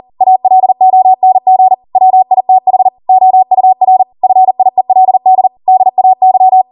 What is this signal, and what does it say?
Morse code, heard as a keyed tone, about 40 wpm, 750 Hz. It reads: ALONG WITH OUR FIELD DAY